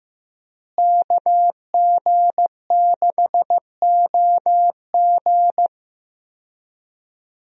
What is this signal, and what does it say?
Morse code, heard as a keyed tone, about 16 wpm, 700 Hz. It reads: KG6OG
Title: USA Call Signs - 5 Characters - Repeated 2x - 15wpm